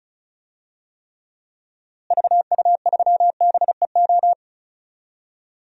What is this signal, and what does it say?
Morse code, heard as a keyed tone, about 34 wpm, 700 Hz. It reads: VU3BEO
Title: DXCC-100 Call Signs - 6 Characters - Repeated 2x - 35wpm